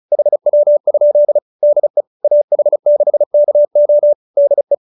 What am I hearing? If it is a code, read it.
Morse code, heard as a keyed tone, about 34 wpm, 590 Hz. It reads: HW? DE AH6KO DE